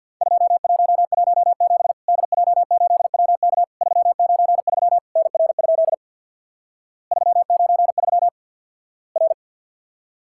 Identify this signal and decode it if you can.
Morse code, heard as a keyed tone, about 50 wpm, 700 Hz. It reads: 2117 DJ8WK 393 NR? 393 R
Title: Worked All Europe DX Contest - QTC Exchange with Repeats - 50wpm